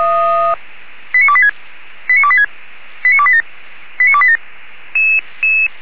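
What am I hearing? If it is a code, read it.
Five-tone selective call (ZVEI), heard as quick runs of five tones, tones of about 80 ms. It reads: 89286 89286 89287 89287
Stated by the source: Received by SDR, ZVEI five-tone selective call.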